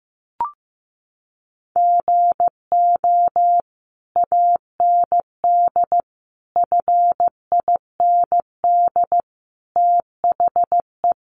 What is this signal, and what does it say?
Morse code, heard as a keyed tone, about 15 wpm, 700 Hz. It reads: GO AND FIND THE
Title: Sentences from Top 1200 Words 15wpm